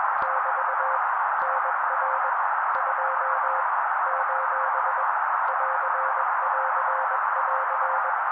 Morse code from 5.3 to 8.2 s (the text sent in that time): <AR><AR><AR>